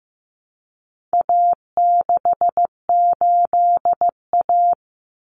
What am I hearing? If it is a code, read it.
Morse code, heard as a keyed tone, about 15 wpm, 700 Hz. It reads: A68A
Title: Common Cut Numbers - 4-Digit 15wpm